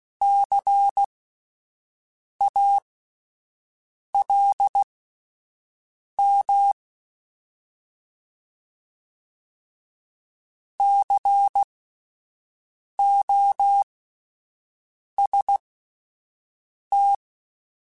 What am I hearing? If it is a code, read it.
Morse code, heard as a keyed tone, about 16 wpm, spaced as for 5 wpm, 800 Hz. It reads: CALM COST
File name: Alphabetical / Alphabetical mp3s / Cs @5